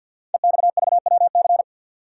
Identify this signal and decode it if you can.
Morse code, heard as a keyed tone, about 51 wpm, 700 Hz. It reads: EXVWC
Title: Code Groups - 5 Letters - 50wpm